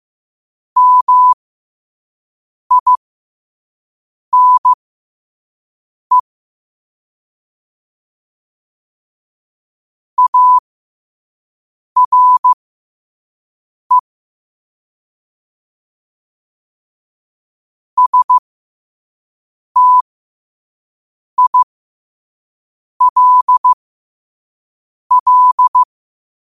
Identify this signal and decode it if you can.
Morse code, heard as a keyed tone, about 15 wpm, spaced as for 5 wpm, 1000 Hz. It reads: MINE ARE STILL
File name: qotdfiles / 5wpm